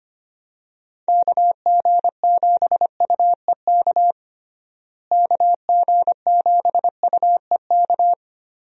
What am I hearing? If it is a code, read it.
Morse code, heard as a keyed tone, about 25 wpm, 700 Hz. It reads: KG7UEK KG7UEK
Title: USA Call Signs - 6 Characters - Repeated 3x - 25wpm